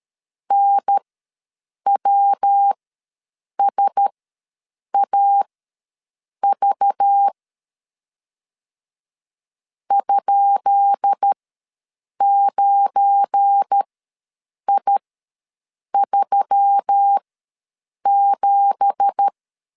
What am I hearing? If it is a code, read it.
Morse code, heard as a keyed tone, about 13 wpm, spaced as for 7 wpm, 800 Hz. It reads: NWSAV ?9I37
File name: LEZIONI / 10 SIGNS / 10B INT ws 12wpm 3527